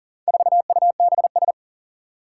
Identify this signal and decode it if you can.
Morse code, heard as a keyed tone, about 40 wpm, 700 Hz. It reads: 4UBS
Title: Four Challenging Letter-Number Combinations 40wpm